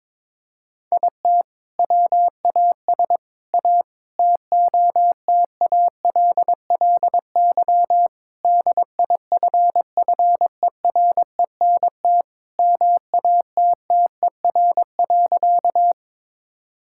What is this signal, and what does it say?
Morse code, heard as a keyed tone, about 22 wpm, 700 Hz. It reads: IT WAS A TOTALLY DIFFERENT MATTER.
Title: Sentences from Top 1600 Words 22wpm